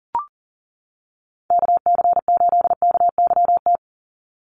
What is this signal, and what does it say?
Morse code, heard as a keyed tone, about 39 wpm, 700 Hz. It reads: KC8KYT
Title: USA Call Signs 40wpm